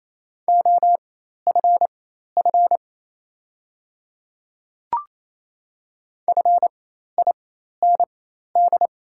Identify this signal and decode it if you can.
Morse code, heard as a keyed tone, about 27 wpm, spaced as for 13 wpm, 690 Hz. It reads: OFF FIND
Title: 4x Character Spacing - Top 200 Words - 28wpm